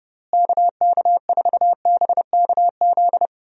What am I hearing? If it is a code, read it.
Morse code, heard as a keyed tone, about 30 wpm, 700 Hz. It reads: KK4BKZ
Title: USA Call Signs - 6 Characters - Repeated 3x - 30wpm